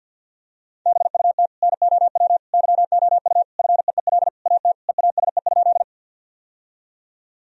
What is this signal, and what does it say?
Morse code, heard as a keyed tone, about 52 wpm, 700 Hz. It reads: BUT NOW YOU FEEL AT EASE?
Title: Sentences from Top 1000 Words - Review Part 1 - 50wpm